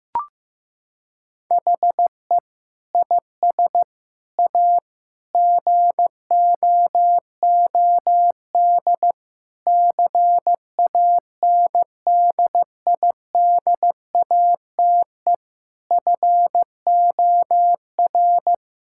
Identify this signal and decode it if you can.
Morse code, heard as a keyed tone, about 16 wpm, 700 Hz. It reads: HE IS A GOOD CANDIDATE FOR
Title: Sentences from Top 1600 Words 15wpm